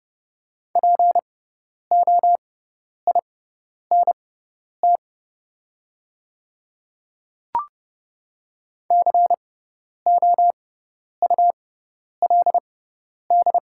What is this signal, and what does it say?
Morse code, heard as a keyed tone, about 30 wpm, spaced as for 10 wpm, 700 Hz. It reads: POINT COULD
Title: 6x Character Spacing - Top 200 Words - 30wpm